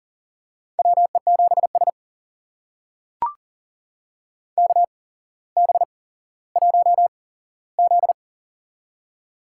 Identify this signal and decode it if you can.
Morse code, heard as a keyed tone, about 40 wpm, spaced as for 17 wpm, 700 Hz. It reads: WE7S KB1Z
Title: USA Call Signs - 4 Characters - 8x Character Spacing - 40wpm